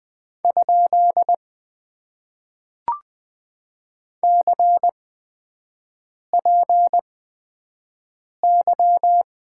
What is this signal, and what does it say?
Morse code, heard as a keyed tone, about 20 wpm, spaced as for 5 wpm, 700 Hz. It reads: ? CPY
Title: Top 90 QSO Elements - 8x Character Spacing - 20wpm